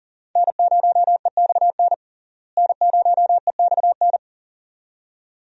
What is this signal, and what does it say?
Morse code, heard as a keyed tone, about 40 wpm, 700 Hz. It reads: N0EXN N0EXN
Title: USA Call Signs - 5 Characters - Repeated 2x - 40wpm